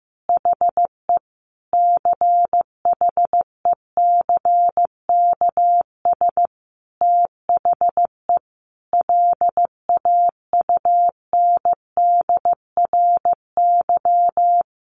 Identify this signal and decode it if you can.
Morse code, heard as a keyed tone, about 15 wpm, 700 Hz. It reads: HE CHECKS THE LAUNDRY
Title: Easy Phrases 15wpm